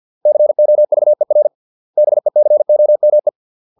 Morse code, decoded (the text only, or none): COVER BECOME